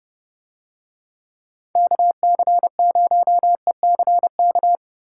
Morse code, heard as a keyed tone, about 30 wpm, 700 Hz. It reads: KC0ECK